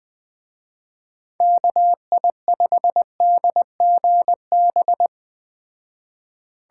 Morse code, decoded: KI5DGB

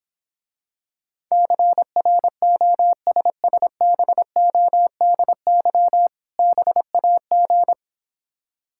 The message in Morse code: CROSSBODY BAG

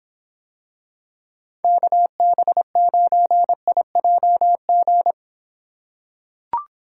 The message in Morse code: KB9IJG

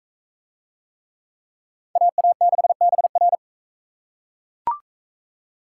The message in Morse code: AA6BR

